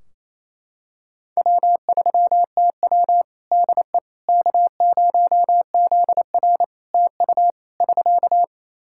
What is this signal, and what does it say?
Morse code, heard as a keyed tone, about 27 wpm, 700 Hz. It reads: W3TW DE K0ZR TU <SK>
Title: K1USN SST QSO Practice 28wpm